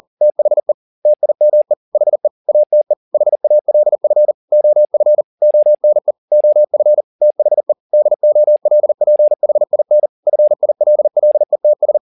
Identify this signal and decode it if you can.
Morse code, heard as a keyed tone, about 40 wpm, 600 Hz. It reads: THE TIME HE ATE HALF OF ONE OF THE DOLPHIN FILLETS
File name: the old man and the sea 2 40wpm